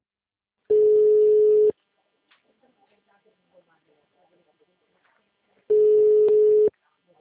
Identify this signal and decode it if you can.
Ringback tone (Europe, 425 Hz), with speech.